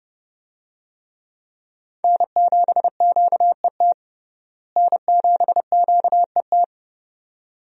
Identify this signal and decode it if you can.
Morse code, heard as a keyed tone, about 30 wpm, 700 Hz. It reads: N7QET N7QET
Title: USA Call Signs - 5 Characters - Repeated 2x - 30wpm